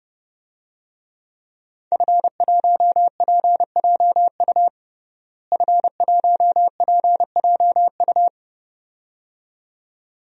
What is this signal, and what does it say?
Morse code, heard as a keyed tone, about 30 wpm, 700 Hz. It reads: F1PJU F1PJU
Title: DXCC-100 Call Signs - 5 Characters - Repeated 2x - 30wpm